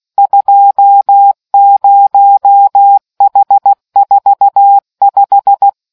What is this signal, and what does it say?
Morse code, heard as a keyed tone, about 16 wpm, 800 Hz. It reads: 20H45